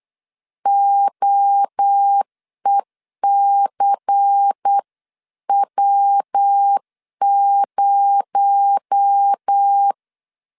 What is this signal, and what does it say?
Morse code, heard as a keyed tone, about 8 wpm, 800 Hz. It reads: OECW0
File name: CW / DET / 1 9 1500C S B 8wpm 5624